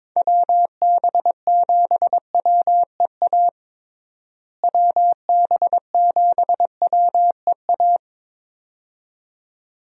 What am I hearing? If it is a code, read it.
Morse code, heard as a keyed tone, about 22 wpm, 700 Hz. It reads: WB7WEA WB7WEA
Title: USA Call Signs - 6 Characters - Repeated 2x - 22wpm